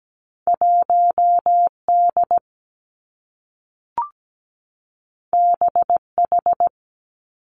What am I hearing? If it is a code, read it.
Morse code, heard as a keyed tone, about 17 wpm, 700 Hz. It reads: 1 D B H
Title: Two Challenging Letter-Number Combinations 17wpm